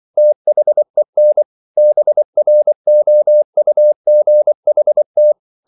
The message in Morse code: THEN BROUGHT

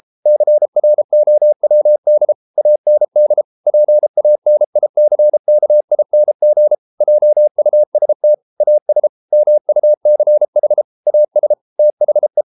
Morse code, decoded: CROWD AND PANICKING JUST AS MUCH AS THE

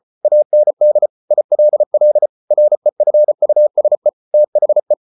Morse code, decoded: AND ILL REFUSE THE

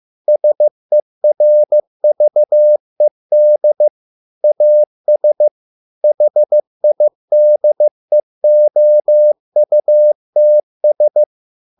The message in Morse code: SERVED AS HIDEOUTS

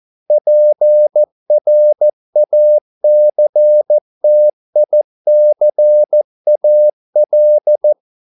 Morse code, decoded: PRACTICAL

